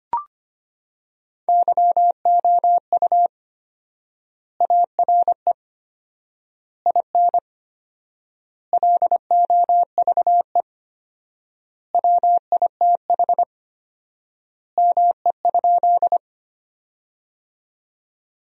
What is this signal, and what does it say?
Morse code, heard as a keyed tone, about 25 wpm, 700 Hz. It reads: YOU ARE IN LOVE WITH ME?
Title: 4x Word-Spacing - Sentences from Top 400 Words - 25wpm